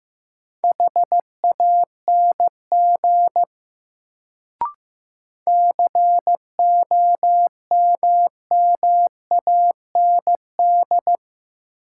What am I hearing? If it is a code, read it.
Morse code, heard as a keyed tone, about 16 wpm, 700 Hz. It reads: HANG COMMAND
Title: Sentences from Top 1600 Words - New Word Drill Alt - 15wpm